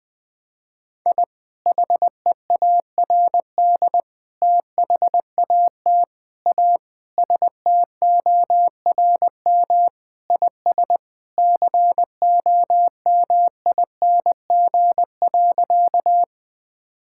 Morse code, heard as a keyed tone, about 20 wpm, 700 Hz. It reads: I HEARD THAT A STORM IS COMING.
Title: Sentences with 2-Word Phrases - Spoken Beginning - 20wpm